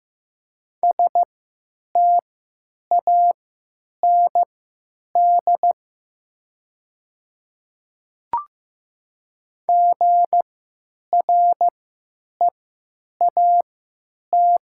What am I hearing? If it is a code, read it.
Morse code, heard as a keyed tone, about 15 wpm, spaced as for 9 wpm, 700 Hz. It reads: STAND GREAT
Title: 3x Character Spacing - Top 300 Words - 15wpm